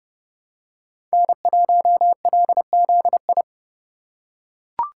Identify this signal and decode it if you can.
Morse code, heard as a keyed tone, about 30 wpm, 700 Hz. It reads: N1LZI
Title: USA Call Signs - 3x Character Spacing - 30wpm